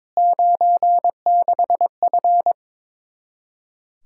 Morse code, heard as a keyed tone, about 21 wpm, 700 Hz. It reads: 96F
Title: Practice QSO - Exchange 2 - Info - 22wpm